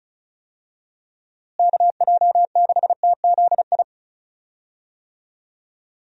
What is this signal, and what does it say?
Morse code, heard as a keyed tone, about 36 wpm, 700 Hz. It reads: KJ6TZI
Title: USA Call Signs - 6 Characters - Repeated 2x - 35wpm